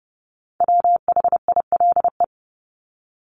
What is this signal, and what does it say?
Morse code, heard as a keyed tone, about 30 wpm, 700 Hz. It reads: WHILE
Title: Top 300 Words Repeated 30wpm